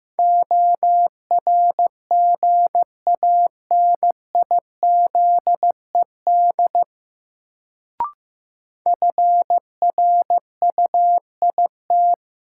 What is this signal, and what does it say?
Morse code, heard as a keyed tone, about 15 wpm, 700 Hz. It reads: ORGANIZED FRUIT
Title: Sentences from Top 1300 Words - New Word Drill - 15wpm